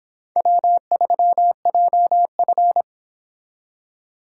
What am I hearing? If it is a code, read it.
Morse code, heard as a keyed tone, about 30 wpm, 700 Hz. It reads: W3JF